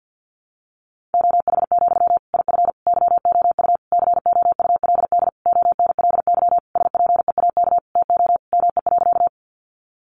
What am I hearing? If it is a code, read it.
Morse code, heard as a keyed tone, about 50 wpm, 700 Hz. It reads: OH, IF YOU COULD ONLY SPEAK TO ME.
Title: Sentences from Top 1100 Words 50wpm